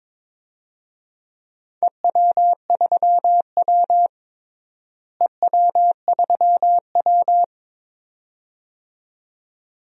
Morse code, heard as a keyed tone, about 22 wpm, 700 Hz. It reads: EW3W EW3W
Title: DXCC-100 Call Signs - 4 Characters - Repeated 2x - 22wpm